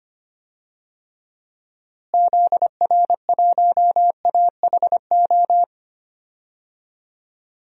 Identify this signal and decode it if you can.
Morse code, heard as a keyed tone, about 25 wpm, 700 Hz. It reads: ZR1AHO
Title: DXCC-100 Call Signs - 6 Characters - Repeated 2x - 25wpm